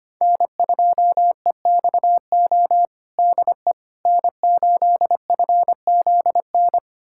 Morse code, heard as a keyed tone, about 25 wpm, 700 Hz. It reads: N2EXO DE N8FZN
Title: USA Callsign DE Callsign - 5 Characters - 25wpm